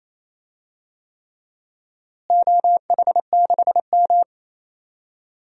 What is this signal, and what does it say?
Morse code, heard as a keyed tone, about 28 wpm, 700 Hz. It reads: OH6M
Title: DXCC-100 Call Signs - 4 Characters - Repeated 3x - 28wpm